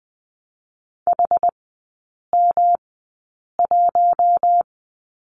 Morse code, heard as a keyed tone, about 20 wpm, spaced as for 8 wpm, 690 Hz. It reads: HM1
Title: 133 Lesson 22e - Question Mark - Three Characters 20wpm